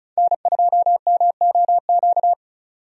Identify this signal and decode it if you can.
Morse code, heard as a keyed tone, about 34 wpm, 700 Hz. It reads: N2MOQ
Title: USA Call Signs - 5 Characters - 35wpm